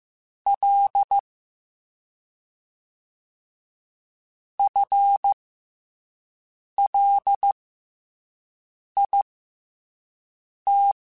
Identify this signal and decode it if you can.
Morse code, heard as a keyed tone, about 15 wpm, spaced as for 5 wpm, 800 Hz. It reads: L FLIT
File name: FL / FL mp3s / FL Prefixes 1x1 @5